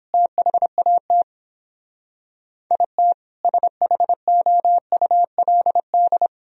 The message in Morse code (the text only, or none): THAT IT SHOULD